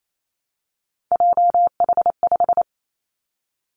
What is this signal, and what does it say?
Morse code, heard as a keyed tone, about 28 wpm, 700 Hz. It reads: JH5
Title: Three Challenging Letter-Number Combinations 28wpm